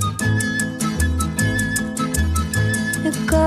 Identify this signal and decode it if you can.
There is no code.